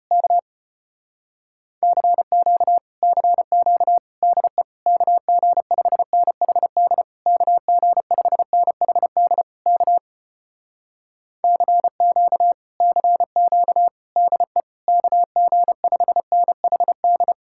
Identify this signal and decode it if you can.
Morse code, heard as a keyed tone, about 33 wpm, 700 Hz. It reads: K CQ CQ DE KG5NHD KG5NHD K CQ CQ DE KG5NHD